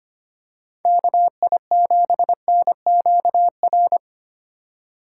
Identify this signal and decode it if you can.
Morse code, heard as a keyed tone, about 25 wpm, 700 Hz. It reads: KI7NQR
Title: USA Call Signs - 6 Characters - 2x Character Spacing - 25wpm